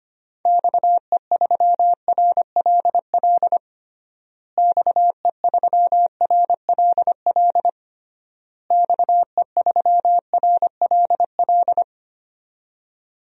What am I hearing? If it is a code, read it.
Morse code, heard as a keyed tone, about 25 wpm, 700 Hz. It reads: XE3RLL XE3RLL XE3RLL
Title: DXCC-100 Call Signs - 6 Characters - Repeated 3x - 25wpm